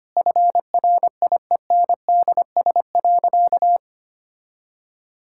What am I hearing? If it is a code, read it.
Morse code, heard as a keyed tone, about 25 wpm, 700 Hz. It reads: FRIENDS.
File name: practice / 25wpm / Sentences with 4-Word Phrases - 25wpm